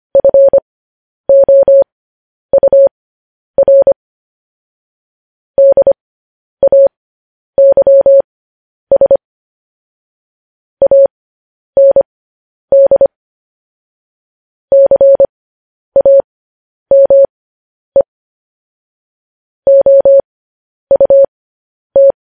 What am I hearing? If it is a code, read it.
Morse code, heard as a keyed tone, about 25 wpm, spaced as for 10 wpm, 550 Hz. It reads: FOUR DAYS AND CAME OUT
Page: News Headlines in Morse Code at 10 WPM